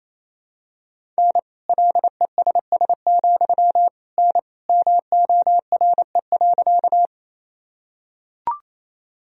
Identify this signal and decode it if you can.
Morse code, heard as a keyed tone, about 28 wpm, 700 Hz. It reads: N LESS, N MORE.
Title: Sentences from Top 400 Words and Abbreviations 28wpm